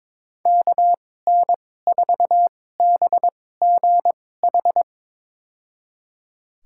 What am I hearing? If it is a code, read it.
Morse code, heard as a keyed tone, about 22 wpm, spaced as for 16 wpm, 700 Hz. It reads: KN4BGH
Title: USA Call Signs - 6 Characters - 2x Character Spacing - 22wpm